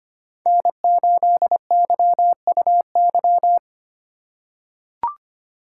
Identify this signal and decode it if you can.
Morse code, heard as a keyed tone, about 25 wpm, 700 Hz. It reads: N8YUY
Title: USA Call Signs - 5 Characters - 2x Character Spacing - 25wpm